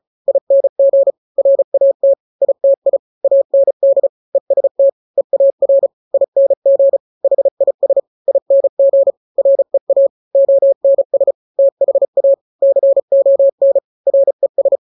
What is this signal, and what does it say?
Morse code, heard as a keyed tone, about 34 wpm, 550 Hz. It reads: ING RAT ITI AND EST EAR ING HIS ING REA ONS THA CON RES
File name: 3-grams 35 30 WPM